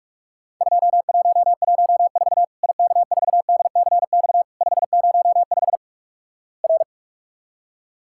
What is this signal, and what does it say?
Morse code, heard as a keyed tone, about 46 wpm, 700 Hz. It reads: 2114 IK4DCX 505 R
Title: Worked All Europe DX Contest - QTC Exchange - 45wpm